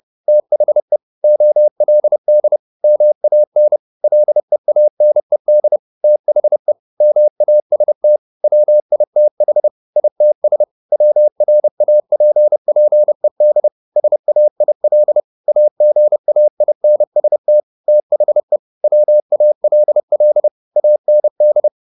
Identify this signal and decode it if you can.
Morse code, heard as a keyed tone, about 30 wpm, 600 Hz. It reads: THE OLD MAN LEANED THE MAST WITH ITS WRAPPED SAIL AGAINST THE WALL AND
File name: the old man and the sea / the old man and the sea 1 30wpm